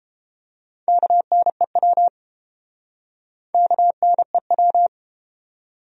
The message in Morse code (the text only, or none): KNEW KNEW